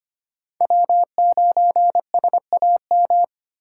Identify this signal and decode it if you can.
Morse code, heard as a keyed tone, about 25 wpm, 700 Hz. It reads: W9SAM
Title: USA Call Signs - 4x Character Spacing - 25wpm